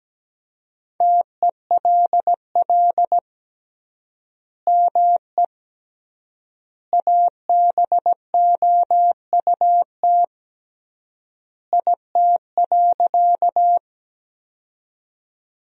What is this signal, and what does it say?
Morse code, heard as a keyed tone, about 17 wpm, 700 Hz. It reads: TELL ME ABOUT IT.
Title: 3x Word-Spacing - Sentences from Top 200 Words - 17wpm